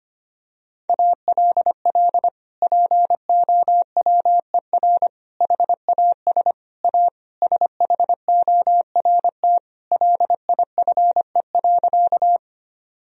Morse code, heard as a keyed tone, about 25 wpm, 700 Hz. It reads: ALL POWER HAS A SHORT LIFE.
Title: Sentences from Top 1000 Words - Review Part 2 - 25wpm